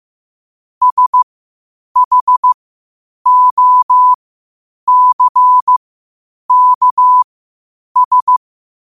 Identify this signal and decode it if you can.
Morse code, heard as a keyed tone, about 15 wpm, spaced as for 8 wpm, 1000 Hz. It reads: SHOCKS
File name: qotdfiles / 8wpm